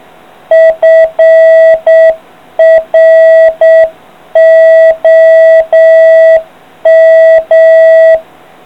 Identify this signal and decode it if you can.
Morse code, heard as a keyed tone, about 7 wpm, 650 Hz. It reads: FROM